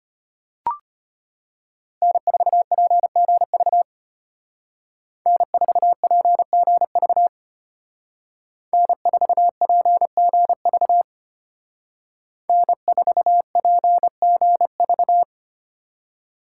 Morse code, heard as a keyed tone, about 33 wpm, 700 Hz. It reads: N4PGV N4PGV N4PGV N4PGV